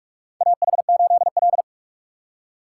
Morse code, heard as a keyed tone, about 44 wpm, 700 Hz. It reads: AH8L
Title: USA Call Signs - 4 Characters - 4x Character Spacing - 45wpm